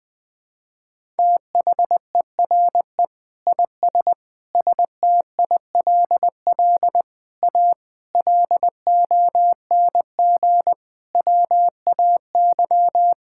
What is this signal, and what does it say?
Morse code, heard as a keyed tone, about 20 wpm, 700 Hz. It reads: THERE IS STILL A LONG WAY
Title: Sets of 8 Words - 20wpm